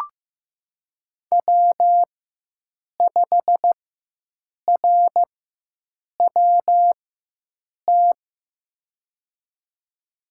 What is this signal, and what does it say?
Morse code, heard as a keyed tone, about 16 wpm, spaced as for 7 wpm, 700 Hz. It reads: W5RWT
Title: USA Call Signs - 4x Character Spacing - 15wpm